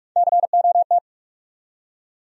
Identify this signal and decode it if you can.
Morse code, heard as a keyed tone, about 46 wpm, 700 Hz. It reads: COT
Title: USA Call Sign Postfixes - 3 Letters 45wpm